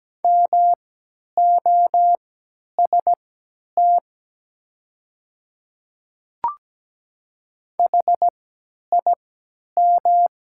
Morse code, heard as a keyed tone, about 17 wpm, spaced as for 10 wpm, 700 Hz. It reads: MOST HIM
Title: 3x Character Spacing - Top 100 Words - 17wpm